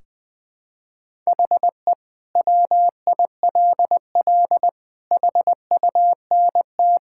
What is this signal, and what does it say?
Morse code, heard as a keyed tone, about 21 wpm, 700 Hz. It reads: HE WILL HUNT
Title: Sentences from Top 1600 Words 20wpm